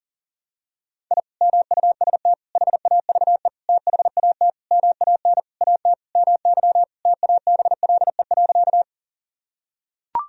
Morse code, heard as a keyed tone, about 40 wpm, 700 Hz. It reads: I MUST HAVE THAT MAN AT MY TABLE.
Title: Sentences from Top 1000 Words - Review Part 2 - 40wpm